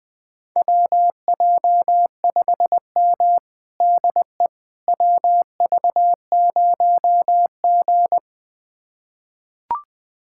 Morse code, heard as a keyed tone, about 20 wpm, 700 Hz. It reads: WJ5M DE WV0G
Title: USA Callsign DE Callsign - 4 Characters - 20wpm